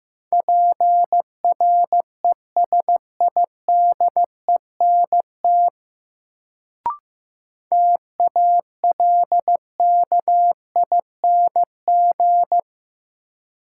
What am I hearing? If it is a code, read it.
Morse code, heard as a keyed tone, about 15 wpm, 700 Hz. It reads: PRESIDENT TALKING
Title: Sets of 2 Words - New Word Drill - 15wpm